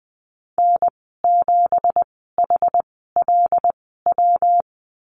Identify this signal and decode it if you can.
Morse code, heard as a keyed tone, about 20 wpm, 700 Hz. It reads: N 7 H L W
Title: 207 Lesson 33g - Letter j - Call Signs 20wpm